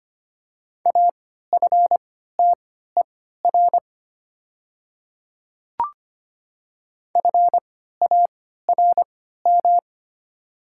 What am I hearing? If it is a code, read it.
Morse code, heard as a keyed tone, about 25 wpm, spaced as for 14 wpm, 690 Hz. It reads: AFTER FARM
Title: 3x Character Spacing - Top 500 Words - 25wpm